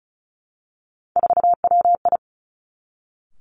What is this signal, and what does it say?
Morse code, heard as a keyed tone, about 36 wpm, 700 Hz. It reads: 4WI
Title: Three Challenging Letter-Number Combinations 35wpm